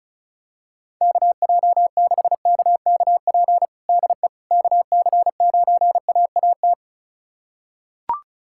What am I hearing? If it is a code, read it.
Morse code, heard as a keyed tone, about 36 wpm, 700 Hz. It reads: KJ6KKP DE KC9AAT
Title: USA Callsign DE Callsign - 6 Characters - 35wpm